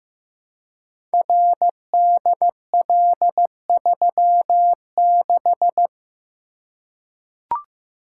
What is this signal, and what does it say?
Morse code, heard as a keyed tone, about 15 wpm, 700 Hz. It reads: RDL36
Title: Code Groups - 5 Letter:Numbers - 15wpm